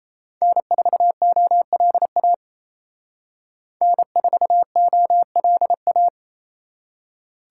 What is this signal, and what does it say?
Morse code, heard as a keyed tone, about 30 wpm, 700 Hz. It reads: N4OLA N4OLA
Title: Speed Racing - USA Call Signs 28wpm